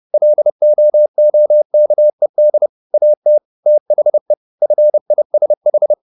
Morse code, heard as a keyed tone, about 30 wpm, 600 Hz. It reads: LOOKED AT THE FISH